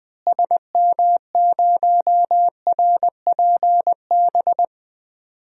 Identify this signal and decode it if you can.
Morse code, heard as a keyed tone, about 20 wpm, 700 Hz. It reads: SM0RPB
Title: DXCC-100 Call Signs - 6 Characters - Repeated 3x - 20wpm